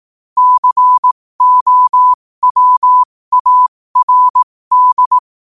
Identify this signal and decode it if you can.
Morse code, heard as a keyed tone, about 18 wpm, spaced as for 15 wpm, 1000 Hz. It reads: COWARD
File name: qotdfiles / 15wpm